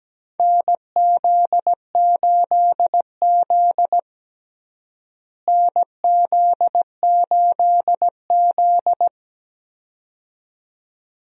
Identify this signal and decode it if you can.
Morse code, heard as a keyed tone, about 17 wpm, 700 Hz. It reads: NZ8Z NZ8Z
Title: USA Call Signs - 4 Characters - Repeated 2x - 17wpm